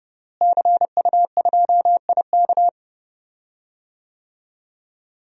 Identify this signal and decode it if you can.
Morse code, heard as a keyed tone, about 30 wpm, 700 Hz. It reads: CU2IK
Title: DXCC-100 Call Signs - 5 Characters - Repeated 2x - 30wpm